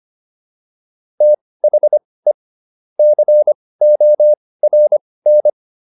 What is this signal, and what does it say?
Morse code, heard as a keyed tone, about 25 wpm, spaced as for 18 wpm, 600 Hz. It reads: THE CORN